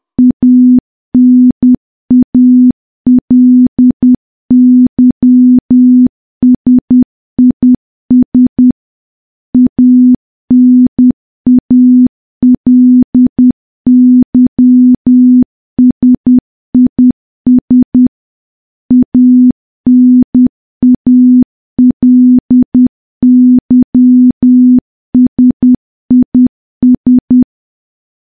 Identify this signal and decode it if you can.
Morse code, heard as a keyed tone, about 10 wpm, 250 Hz. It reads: ANALYSIS ANALYSIS ANALYSIS
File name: morse / morse-training-10WPM